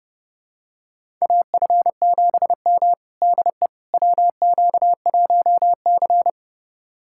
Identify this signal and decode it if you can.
Morse code, heard as a keyed tone, about 30 wpm, 700 Hz. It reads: AF7M DE WQ1C
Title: USA Callsign DE Callsign - 4 Characters - 30wpm